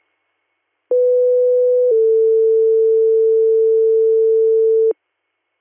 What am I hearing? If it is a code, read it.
Two-tone page, heard as a one-second tone, then a three-second tone, about 500 then 450 Hz. It is heard from 0.9 to 4.9 s.